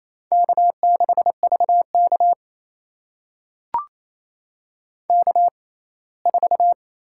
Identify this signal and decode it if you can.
Morse code, heard as a keyed tone, about 28 wpm, spaced as for 19 wpm, 700 Hz. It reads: K6VK K4
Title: USA Call Signs - 4 Characters - 6x Character Spacing - 28wpm